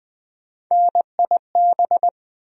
Morse code, heard as a keyed tone, about 20 wpm, 700 Hz. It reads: NIB
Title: USA Call Sign Postfixes - 3 Letters 20wpm